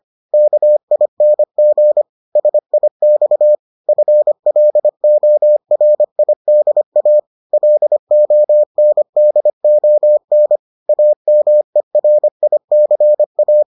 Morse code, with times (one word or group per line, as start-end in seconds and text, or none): 0.3-2.0 KING
2.4-3.5 SIX
3.9-7.2 FLORIDA
7.5-10.6 LONDON
10.9-13.6 AMERICA